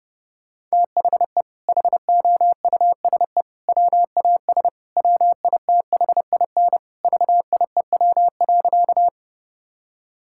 Morse code, heard as a keyed tone, about 30 wpm, 700 Hz. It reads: THE HOUSE WAS WITHIN VIEW.
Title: Sentences from Top 1000 Words - Review Part 6 - 30wpm